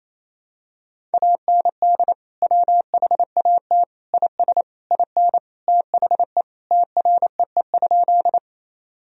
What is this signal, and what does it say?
Morse code, heard as a keyed tone, about 28 wpm, 700 Hz. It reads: AND WHAT IS IN THE TREE?
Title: Sentences from Top 1000 Words - Review Part 1 - 28wpm